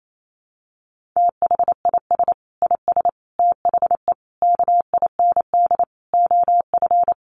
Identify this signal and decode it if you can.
Morse code, heard as a keyed tone, about 28 wpm, 700 Hz. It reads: THIS IS THE KIND OF
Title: Sets of 5 Words Encore 28wpm